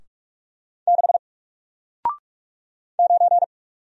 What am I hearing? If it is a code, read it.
Morse code, heard as a keyed tone, about 45 wpm, 700 Hz. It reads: 6 9
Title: Single Character (Letter:Number:Punctuation:Prosign) Alt 45wpm